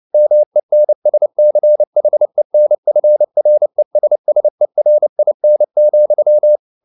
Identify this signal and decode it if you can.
Morse code, heard as a keyed tone, about 29 wpm, 600 Hz. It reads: MENSCHENFRESSERIN,